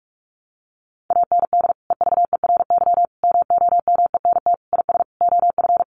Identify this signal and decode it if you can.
Morse code, heard as a keyed tone, about 44 wpm, 700 Hz. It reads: AND EVERY MOMENT IS OF
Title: Sentences from Top 1200 Words 45wpm